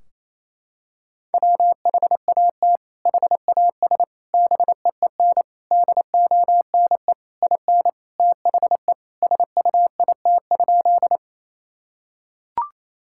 Morse code, heard as a keyed tone, about 28 wpm, 700 Hz. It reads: WHAT HAS BEEN DONE IN THE SUIT?
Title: Sentences from Top 1600 Words 28wpm